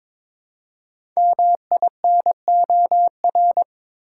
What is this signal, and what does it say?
Morse code, heard as a keyed tone, about 22 wpm, 700 Hz. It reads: MINOR
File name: 5 Letter Words - First 3 Letters Given - 22wpm